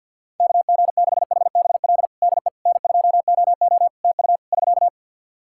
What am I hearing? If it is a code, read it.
Morse code, heard as a keyed tone, about 48 wpm, 700 Hz. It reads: KG6HBL DE N2OO TU <SK>
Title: K1USN SST QSO Practice 50wpm